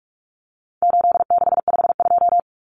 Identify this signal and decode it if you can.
Morse code, heard as a keyed tone, about 44 wpm, 700 Hz. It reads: 8652
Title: Common Cut Numbers - 4-Digit 45wpm